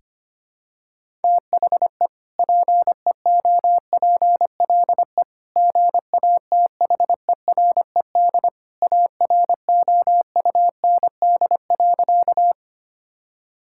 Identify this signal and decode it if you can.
Morse code, heard as a keyed tone, about 25 wpm, 700 Hz. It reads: THE PEOPLE GATHERED AROUND.
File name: practice / 25wpm / Sentences with 2-Word Phrases - 25wpm